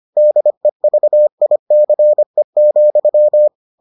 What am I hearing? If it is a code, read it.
Morse code, heard as a keyed tone, about 25 wpm, 600 Hz. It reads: DEVICE,